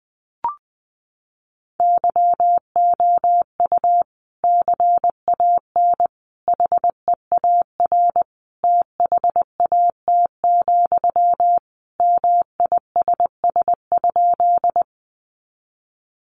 Morse code, heard as a keyed tone, about 19 wpm, 700 Hz. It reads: YOU CAN HEAR THAT, MISS?
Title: Sentences from Top 500 Words 20wpm